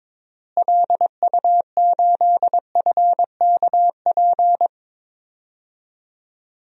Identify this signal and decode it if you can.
Morse code, heard as a keyed tone, about 22 wpm, 700 Hz. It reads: LU8FKP
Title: DXCC-100 Call Signs - 6 Characters - Repeated 3x - 22wpm